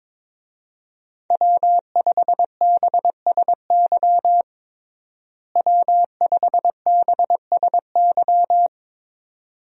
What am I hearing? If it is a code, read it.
Morse code, heard as a keyed tone, about 22 wpm, 700 Hz. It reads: W5BSY W5BSY
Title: USA Call Signs - 5 Characters - Repeated 2x - 22wpm